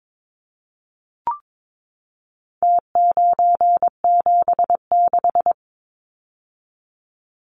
Morse code, heard as a keyed tone, about 22 wpm, 700 Hz. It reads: T976
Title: Numbers - Contest Style - 4-Digit 22wpm